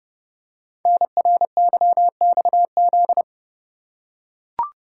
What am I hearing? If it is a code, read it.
Morse code, heard as a keyed tone, about 30 wpm, 700 Hz. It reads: NRYXZ
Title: Code Groups - 5 Letters - 30wpm